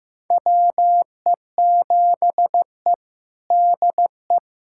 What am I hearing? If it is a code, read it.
Morse code, heard as a keyed tone, about 15 wpm, 700 Hz. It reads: WE7E DE
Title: USA Callsign DE Callsign - 4 Characters - 15wpm